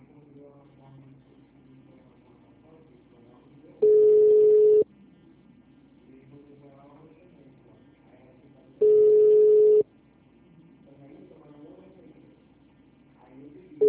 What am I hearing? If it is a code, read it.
Ringback tone (Europe, 425 Hz), with speech.